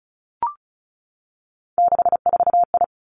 Morse code, heard as a keyed tone, about 36 wpm, 700 Hz. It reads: 64I
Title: Three Challenging Letter-Number Combinations 35wpm